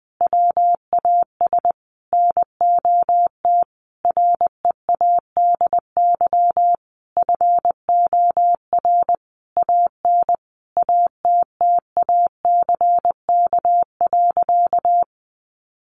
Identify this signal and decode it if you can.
Morse code, heard as a keyed tone, about 20 wpm, 700 Hz. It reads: WAS NOT READY FOR AN ATTACK.
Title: Sentences from Top 1500 Words 20wpm